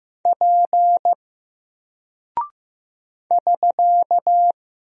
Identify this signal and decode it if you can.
Morse code, heard as a keyed tone, about 15 wpm, 700 Hz. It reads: P <SK>
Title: Single Character (Letter:Number:Punctuation:Prosign) Alt 15wpm